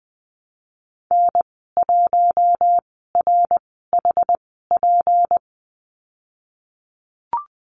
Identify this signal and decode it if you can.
Morse code, heard as a keyed tone, about 19 wpm, 700 Hz. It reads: N 1 R H P
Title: 238 Lesson 38g - Letter q - Call Signs 20wpm